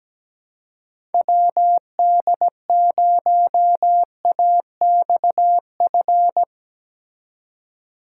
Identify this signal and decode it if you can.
Morse code, heard as a keyed tone, about 18 wpm, 700 Hz. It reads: WD0AXF
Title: Speed Racing - USA Call Signs 17wpm